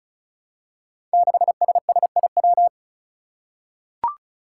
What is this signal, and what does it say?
Morse code, heard as a keyed tone, about 36 wpm, 700 Hz. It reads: 6SSIW
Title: Code Groups - 5 Letter:Numbers - 35wpm